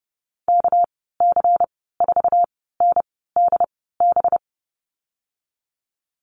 Morse code, heard as a keyed tone, about 30 wpm, spaced as for 17 wpm, 700 Hz. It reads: KC4NDB
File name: Lesson 37g - Letter x - Call Signs 30wpm